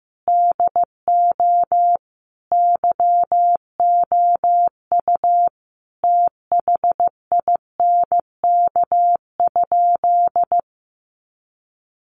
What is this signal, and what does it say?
Morse code, heard as a keyed tone, about 15 wpm, 700 Hz. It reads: DO YOU THINK?
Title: Sentences from Top 200 Words 15wpm